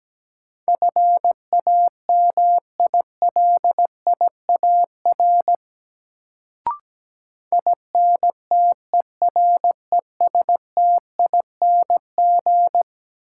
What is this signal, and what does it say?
Morse code, heard as a keyed tone, about 17 wpm, 700 Hz. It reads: FAMILIAR INTERESTING
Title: Sentences from Top 1000 Words - New Word Drill - 17wpm